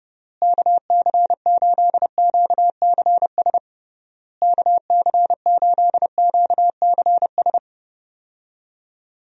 Morse code, heard as a keyed tone, about 30 wpm, 700 Hz. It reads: KC8QCS KC8QCS
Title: USA Call Signs - 6 Characters - Repeated 3x - 30wpm